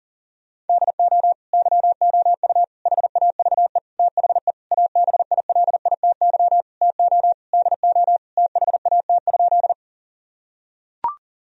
Morse code, heard as a keyed tone, about 40 wpm, 700 Hz. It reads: DO YOU HAVE THE ABILITY TO DO THAT?
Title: Sentences from Top 1000 Words - Review Part 3 - 40wpm